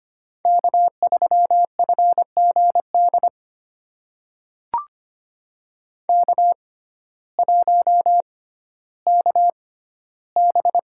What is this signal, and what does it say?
Morse code, heard as a keyed tone, about 25 wpm, spaced as for 14 wpm, 700 Hz. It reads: K3FGD K1KB
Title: USA Call Signs - 6x Character Spacing - 25wpm